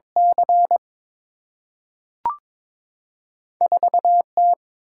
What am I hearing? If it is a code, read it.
Morse code, heard as a keyed tone, about 22 wpm, 700 Hz. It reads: C 4 T
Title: Single-Letter Single-Number 22wpm